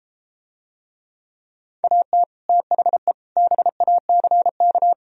Morse code, heard as a keyed tone, about 33 wpm, 700 Hz. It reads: AT THE BACK